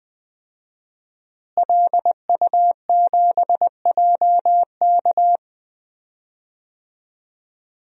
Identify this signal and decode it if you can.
Morse code, heard as a keyed tone, about 21 wpm, 700 Hz. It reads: LU7JK
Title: DXCC-100 Call Signs - 5 Characters - Repeated 2x - 20wpm